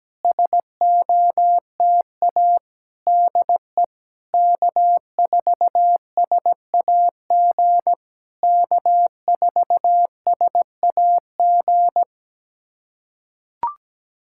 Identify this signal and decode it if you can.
Morse code, heard as a keyed tone, about 17 wpm, 700 Hz. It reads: SOTA DE K4SAG K4SAG
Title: SOTA QSO Practice 17wpm